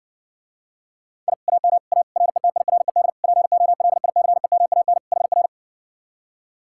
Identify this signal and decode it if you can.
Morse code, heard as a keyed tone, about 60 wpm, 700 Hz. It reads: I AM A RETIRED POLICEMAN HR